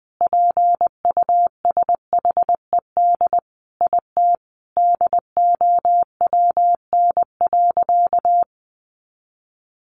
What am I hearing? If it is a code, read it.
Morse code, heard as a keyed tone, about 20 wpm, 700 Hz. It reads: PUSHED IT DOWN.
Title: A Sound Of Thunder 20wpm